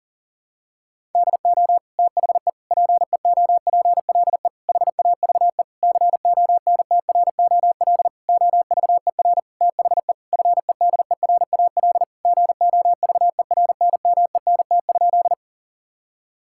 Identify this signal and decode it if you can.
Morse code, heard as a keyed tone, about 40 wpm, 700 Hz. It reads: DO THE PEOPLE HAVE CONTROL OVER THE FEDERAL GOVERNMENT?
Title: Sentences from Top 1000 Words - Review Part 5 - 40wpm